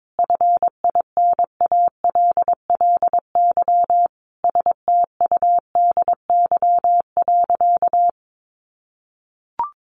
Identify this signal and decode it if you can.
Morse code, heard as a keyed tone, about 22 wpm, 700 Hz. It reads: FINALLY STUDY.
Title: Additional Easy Phrases 22wpm